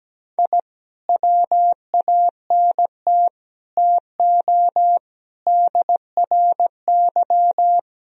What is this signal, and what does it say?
Morse code, heard as a keyed tone, about 17 wpm, 700 Hz. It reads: I WANT TO DRY
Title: Sentences from Top 1000 Words - Review Part 3 - 17wpm